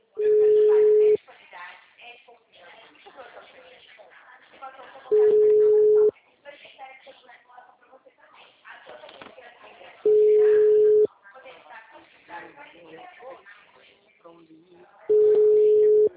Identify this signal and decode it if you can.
Ringback tone (Europe, 425 Hz), with speech.